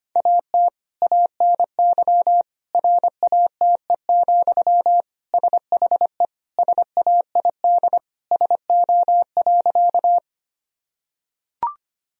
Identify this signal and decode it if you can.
Morse code, heard as a keyed tone, about 25 wpm, 700 Hz. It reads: AT ANY RATE, SHE SAID SO.
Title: Sentences from Top 1000 Words - Review Part 4 - 25wpm